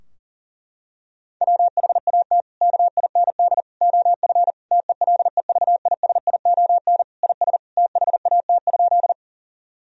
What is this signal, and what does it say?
Morse code, heard as a keyed tone, about 40 wpm, 700 Hz. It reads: WHAT KIND OF TELEVISION IS THAT?
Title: Sentences from Top 1000 Words - Review Part 4 - 40wpm